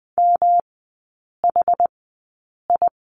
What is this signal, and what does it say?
Morse code, heard as a keyed tone, about 20 wpm, spaced as for 8 wpm, 700 Hz. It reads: MHI
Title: 114 Lesson 19e - Letter w - Three Characters 20wpm